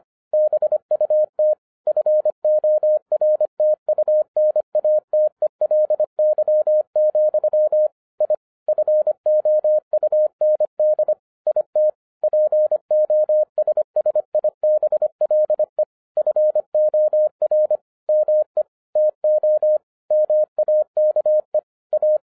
Morse code, heard as a keyed tone, about 25 wpm, 610 Hz. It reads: BUT FORTUNATELY, I FOUND IT POSSIBLE FOR ME TO MAKE A